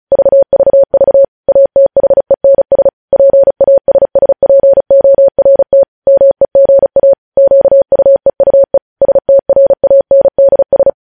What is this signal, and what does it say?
Morse code, heard as a keyed tone, about 36 wpm, 550 Hz. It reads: VVV ATHENS PASSPORT MEGA QUEUE STRANDS
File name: podcast / News headlines-1x-35x35